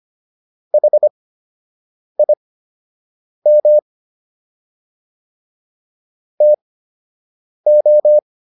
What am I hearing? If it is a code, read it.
Morse code, heard as a keyed tone, about 25 wpm, spaced as for 7 wpm, 600 Hz. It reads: HIM TO